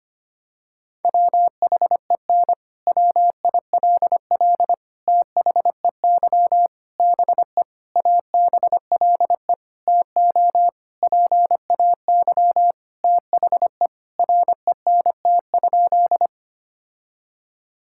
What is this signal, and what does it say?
Morse code, heard as a keyed tone, about 25 wpm, 700 Hz. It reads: WHEN WILL THEY BE ABLE TO PAY THE RENT?
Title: Sentences from Top 1000 Words - Review Part 5 - 25wpm